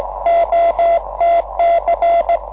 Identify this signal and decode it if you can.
Morse code, heard as a keyed tone, about 18 wpm, 690 Hz. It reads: OTC
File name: 6M audio / vu2rm gi0otc cw cycle23